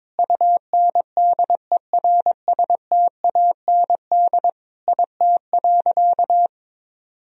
Morse code, heard as a keyed tone, about 21 wpm, 700 Hz. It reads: UNDERSTAND IT.
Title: Sentences from Top 1000 Words - Review Part 3 - 22wpm